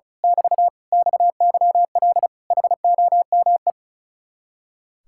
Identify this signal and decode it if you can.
Morse code, heard as a keyed tone, about 33 wpm, 700 Hz. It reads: <BT> XYL HOME
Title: Speed-Racing Common QSO Phrases Encore - 35wpm